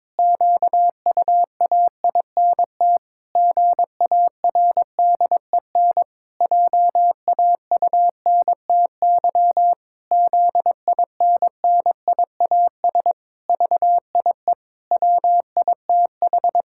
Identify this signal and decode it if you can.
Morse code, heard as a keyed tone, about 22 wpm, 700 Hz. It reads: QUAINT GARDEN JAUNTY ZINNIAS VIE WITH
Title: Pangram Challenge v2 - 22wpm